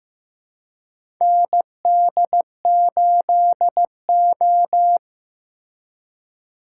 Morse code, heard as a keyed tone, about 15 wpm, 700 Hz. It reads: ND8O
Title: USA Call Signs - 4 Characters - 3x Character Spacing - 15wpm